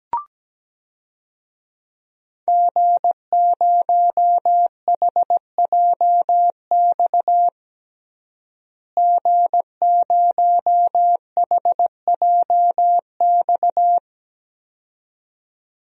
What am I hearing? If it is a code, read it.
Morse code, heard as a keyed tone, about 17 wpm, 700 Hz. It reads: G0HJX G0HJX
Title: DXCC-100 Call Signs - 5 Characters - Repeated 2x - 17wpm